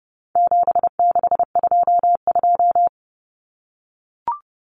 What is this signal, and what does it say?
Morse code, heard as a keyed tone, about 30 wpm, 700 Hz. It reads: 7622
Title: Common Cut Numbers - 4-Digit 30wpm